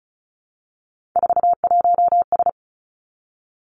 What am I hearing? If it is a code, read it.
Morse code, heard as a keyed tone, about 36 wpm, 700 Hz. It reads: 41S
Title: Three Challenging Letter-Number Combinations 35wpm